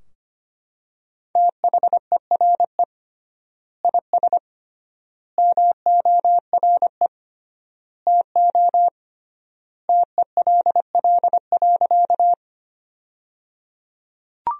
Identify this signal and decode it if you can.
Morse code, heard as a keyed tone, about 25 wpm, 700 Hz. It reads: THERE IS MORE TO TELL.
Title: 3x Word-Spacing - Sentences from Top 200 Words - 25wpm